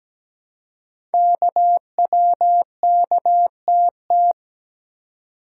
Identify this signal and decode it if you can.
Morse code, heard as a keyed tone, about 17 wpm, 700 Hz. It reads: KWKTT
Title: Code Groups - 5 Letters - 17wpm